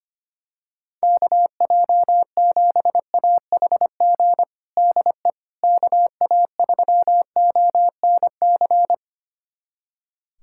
Morse code, heard as a keyed tone, about 25 wpm, 700 Hz. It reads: KJ7AHG DE KA3ONC
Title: USA Callsign DE Callsign 25wpm